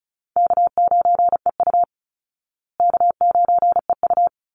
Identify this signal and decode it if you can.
Morse code, heard as a keyed tone, about 34 wpm, 700 Hz. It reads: K9EU K9EU
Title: Call Signs Repeated 3x - 35wpm